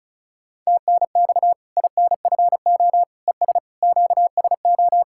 Morse code, heard as a keyed tone, about 34 wpm, 700 Hz. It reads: TNX INFO ES QSO
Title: Practice QSO - Exchange 4 - Ending - 35wpm